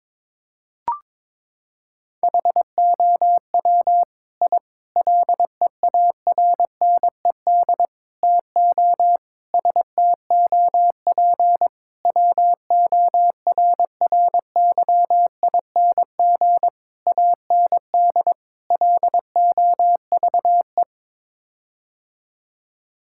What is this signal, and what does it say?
Morse code, heard as a keyed tone, about 22 wpm, 700 Hz. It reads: HOW I LEARNED TO STOP WORRYING AND LOVE
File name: practice / 22wpm / Sets of 8 Words - 22wpm